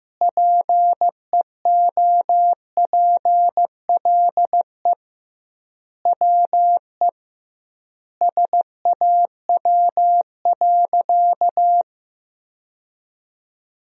Morse code, heard as a keyed tone, about 15 wpm, 700 Hz. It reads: PEOPLE WE SAW.
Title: 2x Word-Spacing - Sentences from Top 300 Words - 15wpm